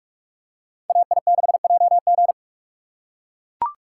Morse code, heard as a keyed tone, about 41 wpm, 700 Hz. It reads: AI6JG